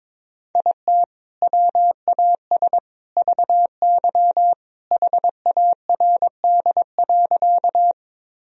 Morse code, heard as a keyed tone, about 22 wpm, 700 Hz. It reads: IT WAS VY HARD.
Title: Sentences from Top 300 Words and Abbreviations 22wpm